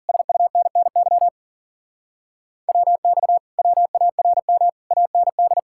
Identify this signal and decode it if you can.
Morse code, heard as a keyed tone, about 50 wpm, 700 Hz. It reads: SUNNY WX WARM AND